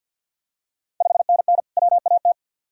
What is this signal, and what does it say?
Morse code, heard as a keyed tone, about 52 wpm, 700 Hz. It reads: 5NN WAT
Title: Ontario QSO Party Exchange 50wpm